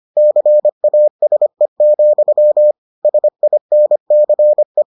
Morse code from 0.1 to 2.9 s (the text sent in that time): CASE,